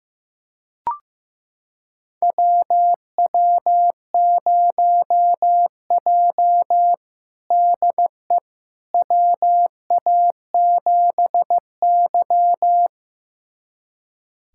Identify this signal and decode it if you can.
Morse code, heard as a keyed tone, about 16 wpm, 700 Hz. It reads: WW0J DE WA7Y
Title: USA Callsign DE Callsign - 4 Characters - 15wpm